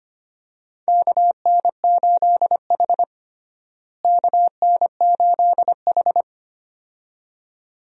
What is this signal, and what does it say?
Morse code, heard as a keyed tone, about 25 wpm, 700 Hz. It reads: KN8H KN8H
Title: USA Call Signs - 4 Characters - Repeated 2x - 25wpm